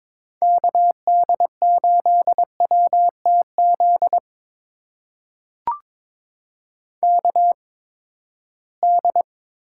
Morse code, heard as a keyed tone, about 22 wpm, spaced as for 15 wpm, 700 Hz. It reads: KD8WTZ KD
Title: USA Call Signs - 6 Characters - 8x Character Spacing - 22wpm